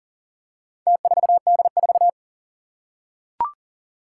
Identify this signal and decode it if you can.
Morse code, heard as a keyed tone, about 37 wpm, 700 Hz. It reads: T4D4
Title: Four Challenging Letter-Number Combinations 40wpm